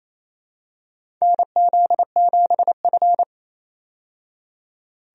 Morse code, heard as a keyed tone, about 28 wpm, 700 Hz. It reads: NZ7F
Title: USA Call Signs - 4 Characters - 28wpm